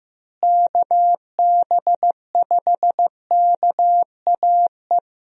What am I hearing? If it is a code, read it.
Morse code, heard as a keyed tone, about 15 wpm, 700 Hz. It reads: KB5KAE
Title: USA Call Signs - 6 Characters - Repeated 2x - 15wpm